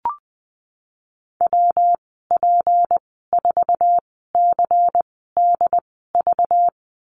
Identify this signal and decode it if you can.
Morse code, heard as a keyed tone, about 20 wpm, spaced as for 15 wpm, 700 Hz. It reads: WP4CDV
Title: 193 Lesson 31g - Letter v - Call Signs 20wpm